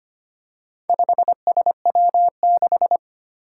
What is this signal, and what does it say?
Morse code, heard as a keyed tone, about 25 wpm, 700 Hz. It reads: 5SW6
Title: Four Challenging Letter-Number Combinations 25wpm